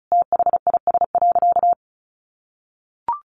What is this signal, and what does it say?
Morse code, heard as a keyed tone, about 34 wpm, 700 Hz. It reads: THIS.
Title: Easy Phrases 35wpm